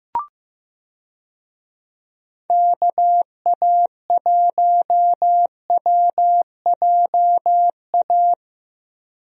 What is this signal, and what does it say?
Morse code, heard as a keyed tone, about 15 wpm, 700 Hz. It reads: KA1WJA
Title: USA Call Signs - 6 Characters - Repeated 3x - 15wpm